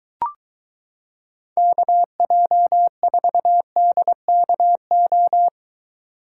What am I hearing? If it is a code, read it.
Morse code, heard as a keyed tone, about 23 wpm, 700 Hz. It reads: KJ4DKO